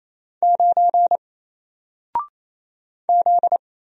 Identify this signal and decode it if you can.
Morse code, heard as a keyed tone, about 27 wpm, 700 Hz. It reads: 9 Z
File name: Single-Letter-Number-28wpm